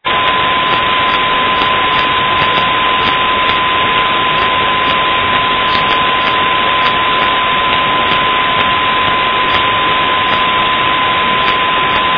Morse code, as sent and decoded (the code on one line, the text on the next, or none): none